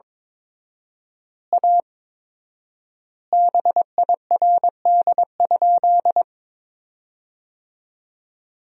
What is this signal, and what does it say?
Morse code, heard as a keyed tone, about 22 wpm, 700 Hz. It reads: A BIRD?
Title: 4x Word-Spacing - Sentences from Top 400 Words - 22wpm